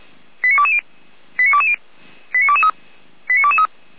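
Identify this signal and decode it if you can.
Five-tone selective call (ZVEI), heard as quick runs of five tones, tones of about 70 ms. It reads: 89220 89220 89223 89223
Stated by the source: ZVEI five-tone selective call received with an SDR, on the audio server of Feuerwehr Orlamünde (fire brigade).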